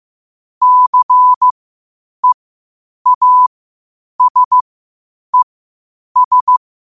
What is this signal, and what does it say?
Morse code, heard as a keyed tone, about 15 wpm, spaced as for 8 wpm, 1000 Hz. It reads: CEASES
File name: qotdfiles / 8wpm